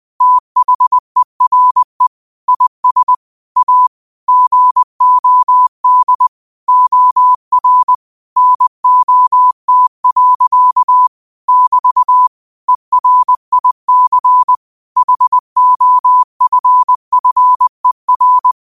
Morse code, heard as a keyed tone, about 20 wpm, 1000 Hz. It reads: THERE IS A GOD OR NOT. <BT> ERIC HOFFER
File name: qotdfiles / 20wpm